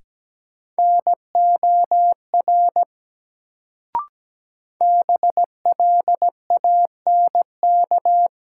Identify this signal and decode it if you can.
Morse code, heard as a keyed tone, about 17 wpm, 700 Hz. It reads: NOR BLANK
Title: Sentences from Top 1500 Words - New Word Drill Alt - 17wpm